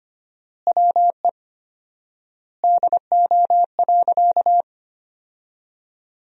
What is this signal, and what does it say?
Morse code, heard as a keyed tone, about 25 wpm, 700 Hz. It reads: WE DO.
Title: 4x Word-Spacing - Sentences from Top 100 Words - 25wpm